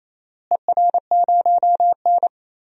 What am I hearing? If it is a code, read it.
Morse code, heard as a keyed tone, about 28 wpm, 700 Hz. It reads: ER0N
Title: DXCC-100 Call Signs - 4 Characters - Repeated 2x - 28wpm